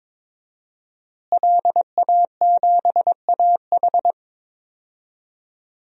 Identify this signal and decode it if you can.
Morse code, heard as a keyed tone, about 21 wpm, 700 Hz. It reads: LA7AH
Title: DXCC-100 Call Signs - 5 Characters - Repeated 2x - 22wpm